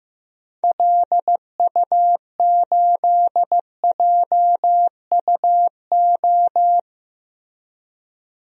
Morse code, heard as a keyed tone, about 15 wpm, 700 Hz. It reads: LU8JUO
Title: DXCC-100 Call Signs - 6 Characters - Repeated 3x - 15wpm